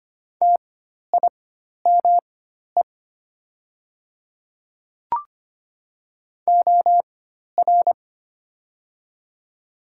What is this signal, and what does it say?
Morse code, heard as a keyed tone, about 25 wpm, spaced as for 12 wpm, 700 Hz. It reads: TIME OR
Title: 4x Character Spacing - Top 100 Words - 25wpm